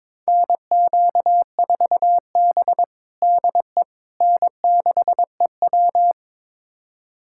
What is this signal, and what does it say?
Morse code, heard as a keyed tone, about 21 wpm, 700 Hz. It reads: NQ4B DE N6EW
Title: USA Callsign DE Callsign - 4 Characters - 22wpm